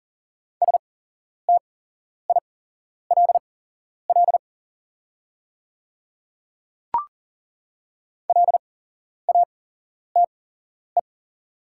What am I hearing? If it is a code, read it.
Morse code, heard as a keyed tone, about 40 wpm, spaced as for 11 wpm, 690 Hz. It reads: STILL LATE
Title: 8x Character Spacing - Top 300 Words - 40wpm